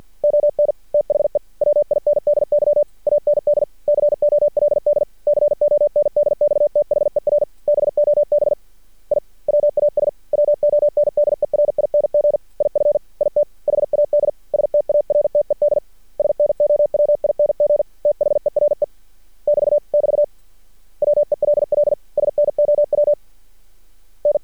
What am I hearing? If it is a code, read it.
Morse code, heard as a keyed tone, about 51 wpm, 590 Hz. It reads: ON THE WINDY AND COLD CONDX THER BOB I WAS WONDERING IF IT HAD STARTED SNOWING THERE <BT> <BT> WELL SNOW N
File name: archived / CRSnet